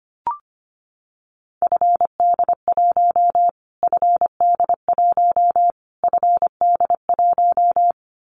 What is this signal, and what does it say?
Morse code, heard as a keyed tone, about 25 wpm, 700 Hz. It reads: FD1 FD1 FD1
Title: Callsign Prefixes for Top 25 DXCC - Repeated 3x - 25wpm